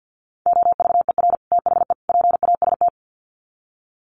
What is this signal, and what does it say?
Morse code, heard as a keyed tone, about 50 wpm, 700 Hz. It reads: OVER THE PAST
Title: Sets of 3 Words 50wpm